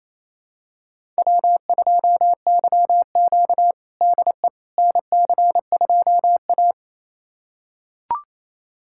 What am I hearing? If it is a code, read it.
Morse code, heard as a keyed tone, about 28 wpm, 700 Hz. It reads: W2YQ DE NC2A
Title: USA Callsign DE Callsign - 4 Characters - 28wpm